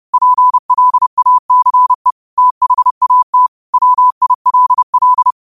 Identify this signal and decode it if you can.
Morse code, heard as a keyed tone, about 30 wpm, 1000 Hz. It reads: PLACE THAT WILL
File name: qotdfiles / 30wpm